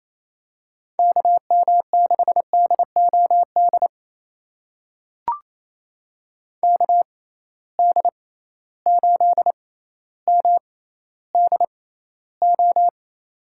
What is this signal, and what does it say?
Morse code, heard as a keyed tone, about 28 wpm, spaced as for 14 wpm, 700 Hz. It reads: KM6DOD KD8MDO
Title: USA Call Signs - 6 Characters - 6x Character Spacing - 28wpm